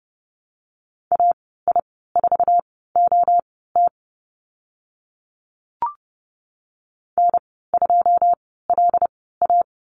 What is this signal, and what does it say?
Morse code, heard as a keyed tone, about 30 wpm, spaced as for 17 wpm, 700 Hz. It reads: AI4OT N2LA
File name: Lesson 15g - Number 5 - Call Signs 30wpm